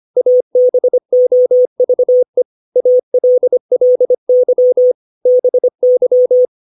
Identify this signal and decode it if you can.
Morse code, heard as a keyed tone, about 25 wpm, 500 Hz. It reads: ABOVE ALLY BY
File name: 632 words / 632 Words 25WPM